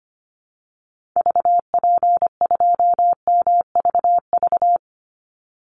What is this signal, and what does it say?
Morse code, heard as a keyed tone, about 25 wpm, 700 Hz. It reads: VP2MVV
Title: Call Signs - 25wpm